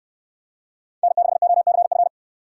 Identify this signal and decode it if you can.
Morse code, heard as a keyed tone, about 64 wpm, 700 Hz. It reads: N6OCF